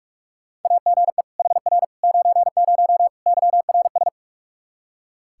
Morse code, heard as a keyed tone, about 43 wpm, 700 Hz. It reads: AGE HR 90 YRS
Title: Practice QSO - Exchange 3 - More Info - 45wpm